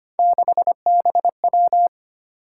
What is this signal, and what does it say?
Morse code, heard as a keyed tone, about 30 wpm, 700 Hz. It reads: 6BW
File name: Speed-Racing - Three Letter-Number Combinations - 25wpm